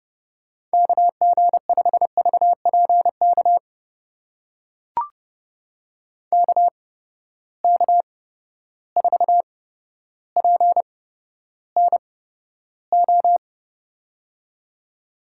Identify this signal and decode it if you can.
Morse code, heard as a keyed tone, about 30 wpm, spaced as for 13 wpm, 700 Hz. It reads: KG5VPK KK4PNO
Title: USA Call Signs - 6 Characters - 8x Character Spacing - 30wpm